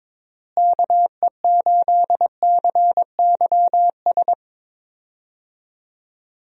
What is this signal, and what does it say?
Morse code, heard as a keyed tone, about 22 wpm, 700 Hz. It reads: KE8CYS
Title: USA Call Signs - 6 Characters - Repeated 2x - 22wpm